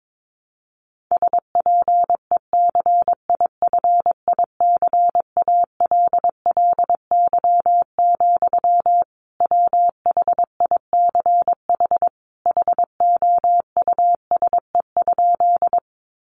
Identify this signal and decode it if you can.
Morse code, heard as a keyed tone, about 22 wpm, 700 Hz. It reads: SPECIFICALLY, WHICH HOUSE?
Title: Sentences from Top 1100 Words 22wpm